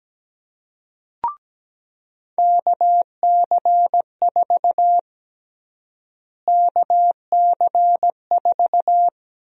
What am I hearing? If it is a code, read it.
Morse code, heard as a keyed tone, about 17 wpm, 700 Hz. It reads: KC4 KC4
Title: USA Call Sign Prefixes - Repeated 2x - 17wpm